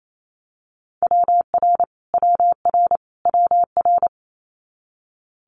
Morse code, heard as a keyed tone, about 28 wpm, 700 Hz. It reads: WR WR WR
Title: Multi-letter Phonemes Repeated 3x 28wpm